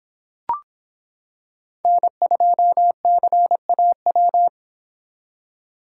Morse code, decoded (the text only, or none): N2CAW